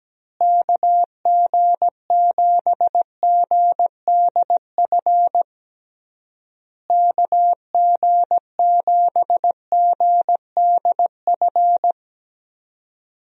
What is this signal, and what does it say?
Morse code, heard as a keyed tone, about 17 wpm, 700 Hz. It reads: KG7GDF KG7GDF
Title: USA Call Signs - 6 Characters - Repeated 3x - 17wpm